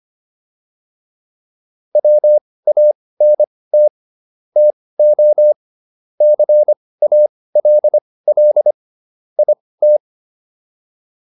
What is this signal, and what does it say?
Morse code, heard as a keyed tone, about 25 wpm, spaced as for 18 wpm, 600 Hz. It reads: WANT TO CALL IT